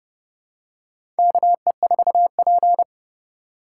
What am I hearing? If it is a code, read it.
Morse code, heard as a keyed tone, about 30 wpm, 700 Hz. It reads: KE4P
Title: USA Call Signs - 4x Character Spacing - 30wpm